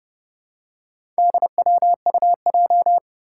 Morse code, heard as a keyed tone, about 29 wpm, 700 Hz. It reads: DWUJ
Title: Four Challenging Letter-Number Combinations 30wpm